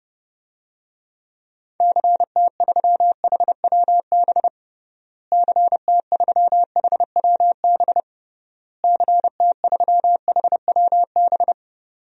Morse code, heard as a keyed tone, about 32 wpm, 700 Hz. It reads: CT3HWB CT3HWB CT3HWB
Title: DXCC-100 Call Signs - 6 Characters - Repeated 3x - 30wpm